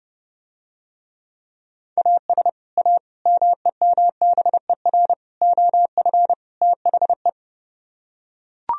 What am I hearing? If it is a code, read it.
Morse code, heard as a keyed tone, about 30 wpm, 700 Hz. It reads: AS A MEMBER OF THE